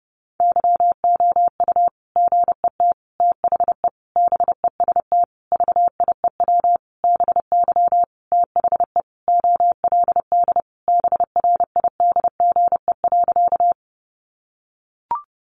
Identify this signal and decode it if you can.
Morse code, heard as a keyed tone, about 30 wpm, 700 Hz. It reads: YOU GET THE BEST VIEW BY THE OLD BRIDGE.
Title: Sentences from Top 1200 Words 30wpm